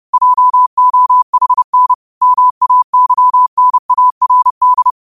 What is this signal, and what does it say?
Morse code, heard as a keyed tone, about 30 wpm, 1000 Hz. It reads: JOHN MAYNARD
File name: qotdfiles / 30wpm